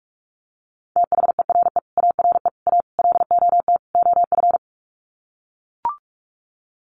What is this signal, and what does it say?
Morse code, heard as a keyed tone, about 44 wpm, 700 Hz. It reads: THERE ARE A LOT OF
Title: Sets of 5 Words 45wpm